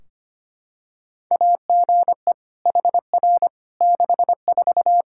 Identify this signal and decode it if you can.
Morse code, heard as a keyed tone, about 25 wpm, 700 Hz. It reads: AGE HR 64
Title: Practice QSO - Exchange 3 - More Info - 25wpm